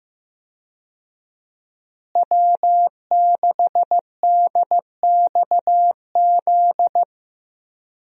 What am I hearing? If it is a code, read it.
Morse code, heard as a keyed tone, about 15 wpm, 700 Hz. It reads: W6DXZ
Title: USA Call Signs - 5 Characters - Repeated 2x - 15wpm